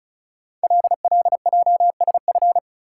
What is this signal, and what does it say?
Morse code, heard as a keyed tone, about 36 wpm, 700 Hz. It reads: LLJSF
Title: Code Groups - 5 Letters - 35wpm